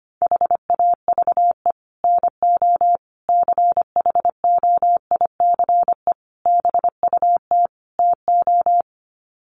Morse code, heard as a keyed tone, about 25 wpm, 700 Hz. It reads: HAVE NO CHOICE BUT TO
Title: Sets of 5 Words Encore 25wpm